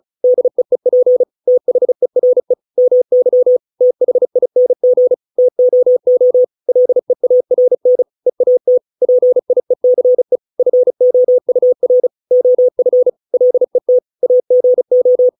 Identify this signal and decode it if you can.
Morse code, heard as a keyed tone, about 34 wpm, 500 Hz. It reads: DEEP THERE MY THING TOO LEARN EAT PIECE FOUR OF LET AGO